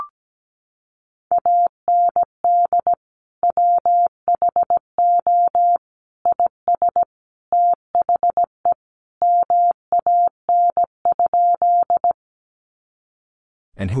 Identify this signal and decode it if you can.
Morse code, heard as a keyed tone, about 17 wpm, 700 Hz. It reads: AND WHO IS THE MAN?
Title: Sentences from Top 200 Words 17wpm